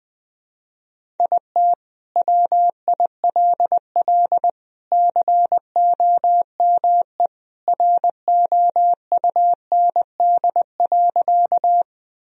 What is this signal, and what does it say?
Morse code, heard as a keyed tone, about 20 wpm, 700 Hz. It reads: IT WILL COME ROUND.
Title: Sentences from Top 1000 Words - Review Part 1 - 20wpm